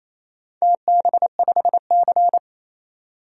Morse code, heard as a keyed tone, about 28 wpm, 690 Hz. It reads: TB5C
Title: DXCC-100 Call Signs - 4 Characters - 28wpm